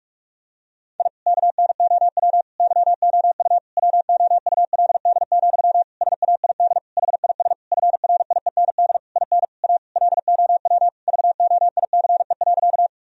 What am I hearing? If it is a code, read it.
Morse code, heard as a keyed tone, about 41 wpm, 700 Hz. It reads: I KNOW YOU WOULD, SAID HIS FRIEND IN A LOW VOICE.
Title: Sentences from Top 1000 Words - Review Part 1 - 45wpm